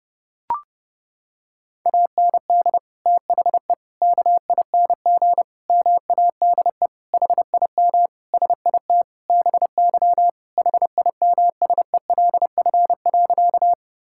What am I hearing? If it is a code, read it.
Morse code, heard as a keyed tone, about 30 wpm, 700 Hz. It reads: AND THE KING MADE HIM SIT BY HIMSELF.
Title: Sentences from Top 1600 Words 30wpm